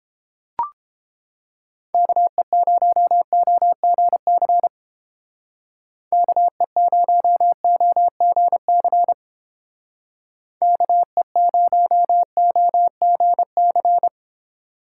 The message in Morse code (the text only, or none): KE0OGC KE0OGC KE0OGC